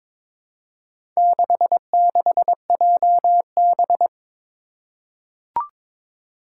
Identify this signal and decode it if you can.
Morse code, heard as a keyed tone, about 22 wpm, 700 Hz. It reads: 66JB
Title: Four Challenging Letter-Number Combinations 22wpm